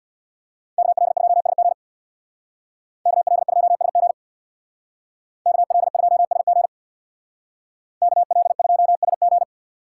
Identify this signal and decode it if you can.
Morse code, heard as a keyed tone, about 62 wpm, 700 Hz. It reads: KL2SG KL2SG KL2SG KL2SG
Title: Speed Racing - USA Call Signs 50wpm